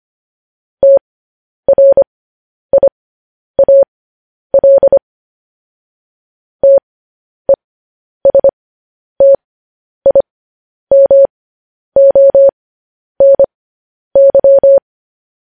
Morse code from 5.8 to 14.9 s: TESTIMONY

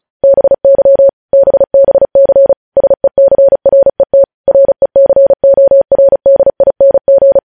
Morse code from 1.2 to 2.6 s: BBC

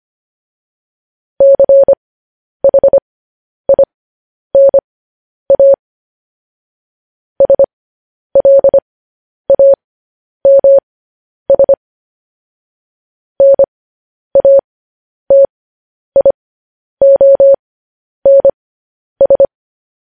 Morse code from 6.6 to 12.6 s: SLAMS

